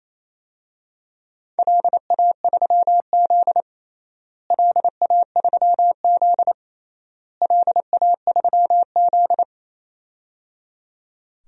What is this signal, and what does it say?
Morse code, heard as a keyed tone, about 28 wpm, 700 Hz. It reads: LA3Z LA3Z LA3Z
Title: DXCC-100 Call Signs - 4 Characters - Repeated 3x - 28wpm